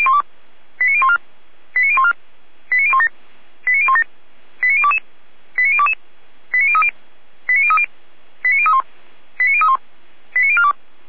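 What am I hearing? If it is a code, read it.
Five-tone selective call (ZVEI), heard as quick runs of five tones, tones of about 70 ms. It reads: 89014 89014 89017 89017 89022 89022 89030 89030 89031 89031 89042
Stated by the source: Five-tone ZVEI selective call received by SDR.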